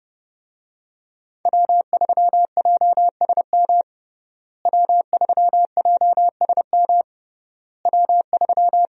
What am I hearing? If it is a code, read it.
Morse code, heard as a keyed tone, about 32 wpm, 700 Hz. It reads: W3JSM W3JSM W3
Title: USA Call Signs - 5 Characters - Repeated 3x - 30wpm